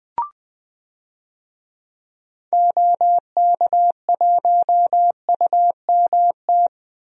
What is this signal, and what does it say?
Morse code, heard as a keyed tone, about 20 wpm, 700 Hz. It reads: OK1UMT
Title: DXCC-100 Call Signs - 6 Characters - Repeated 3x - 20wpm